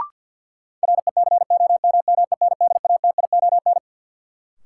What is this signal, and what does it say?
Morse code, heard as a keyed tone, about 53 wpm, 700 Hz. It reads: RECOMMENDATION
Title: Sentences from Top 1400 Words - New Word Drill Alt - 50wpm